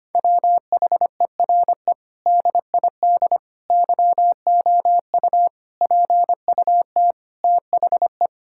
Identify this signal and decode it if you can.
Morse code, heard as a keyed tone, about 25 wpm, 700 Hz. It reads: WHERE DID YOU PUT THE
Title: Sentences from Top 1000 Words - Review Part 6 - 25wpm